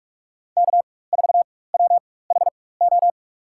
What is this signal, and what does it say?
Morse code, heard as a keyed tone, about 43 wpm, spaced as for 21 wpm, 690 Hz. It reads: K4WHO
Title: USA Call Signs - 4x Character Spacing - 45wpm